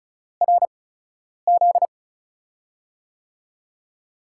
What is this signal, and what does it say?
Morse code, heard as a keyed tone, about 36 wpm, spaced as for 10 wpm, 700 Hz. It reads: RZ
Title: USA Call Signs - 6 Characters - 8x Character Spacing - 35wpm